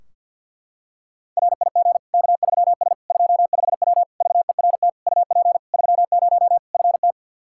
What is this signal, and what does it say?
Morse code, heard as a keyed tone, about 51 wpm, 700 Hz. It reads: RIG K3S 25W VERT UP 30 FT
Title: Practice QSO - Exchange 2 - Info - 50wpm